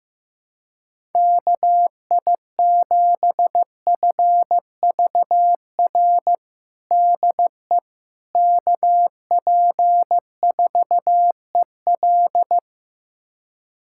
Morse code, heard as a keyed tone, about 16 wpm, 700 Hz. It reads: KI7FVR DE KP4EL
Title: USA Callsign DE Callsign 15wpm